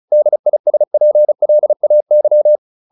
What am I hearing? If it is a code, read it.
Morse code, heard as a keyed tone, about 34 wpm, 600 Hz. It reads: DISPLAY